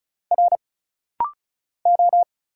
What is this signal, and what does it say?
Morse code, heard as a keyed tone, about 35 wpm, 700 Hz. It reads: R O
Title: Single Letter-Number - Rapid-Fire 35wpm